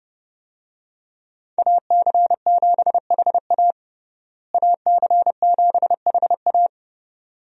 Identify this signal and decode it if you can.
Morse code, heard as a keyed tone, about 32 wpm, 700 Hz. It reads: AC7HA AC7HA
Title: USA Call Signs - 5 Characters - Repeated 2x - 30wpm